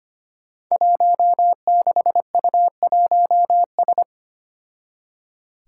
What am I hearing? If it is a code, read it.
Morse code, heard as a keyed tone, about 25 wpm, 700 Hz. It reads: 16U1S
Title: Five Challenging Letter-Number Combinations 25wpm